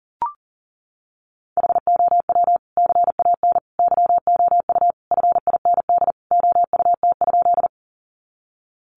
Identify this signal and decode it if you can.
Morse code, heard as a keyed tone, about 37 wpm, 700 Hz. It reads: HOW CAN YOU FIND OUT?
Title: Sentences from Top 200 Words 40wpm